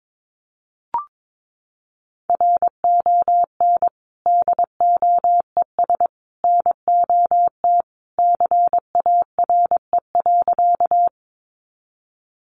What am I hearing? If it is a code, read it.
Morse code, heard as a keyed tone, about 22 wpm, 700 Hz. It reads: RON DOES NOT CARE.
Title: Additional Easy Phrases 22wpm